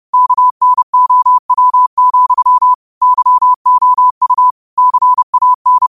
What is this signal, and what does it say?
Morse code, heard as a keyed tone, about 30 wpm, 1000 Hz. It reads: KNOW, YOU CAN
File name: qotdfiles / 30wpm